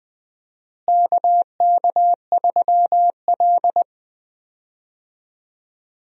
Morse code, heard as a keyed tone, about 20 wpm, 700 Hz. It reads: KK3L
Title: USA Call Signs - 4 Characters - Repeated 2x - 20wpm